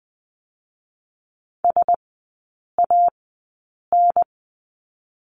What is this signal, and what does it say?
Morse code, heard as a keyed tone, about 20 wpm, 700 Hz. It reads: S A N
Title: 100 Lesson 17e - Letter u - Three Characters 20wpm